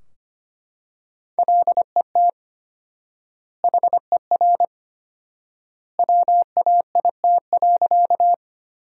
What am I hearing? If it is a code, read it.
Morse code, heard as a keyed tone, about 25 wpm, 700 Hz. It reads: LET HER WAIT.
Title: 4x Word-Spacing - Sentences from Top 500 Words - 25wpm